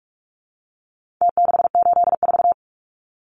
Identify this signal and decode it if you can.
Morse code, heard as a keyed tone, about 44 wpm, 700 Hz. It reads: T684
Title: Numbers - Contest Style - 4-Digit 45wpm